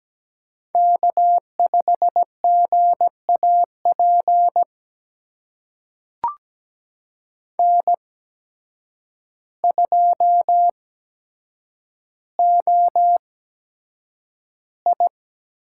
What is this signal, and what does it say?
Morse code, heard as a keyed tone, about 17 wpm, spaced as for 8 wpm, 700 Hz. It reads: K5GAP N2OI
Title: USA Call Signs - 5 Characters - 8x Character Spacing - 17wpm